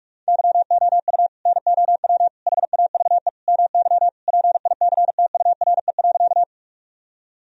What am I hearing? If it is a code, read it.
Morse code, heard as a keyed tone, about 46 wpm, 700 Hz. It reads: YOU NOW HAVE MY PICTURE.
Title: Sentences from Top 1000 Words - Review Part 1 - 45wpm